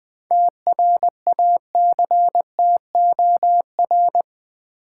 Morse code, heard as a keyed tone, about 20 wpm, 700 Hz. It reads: TRACTOR
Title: Easy Words with Hints - 20wpm